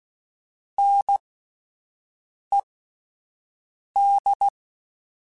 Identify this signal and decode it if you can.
Morse code, heard as a keyed tone, about 16 wpm, spaced as for 5 wpm, 800 Hz. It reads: NED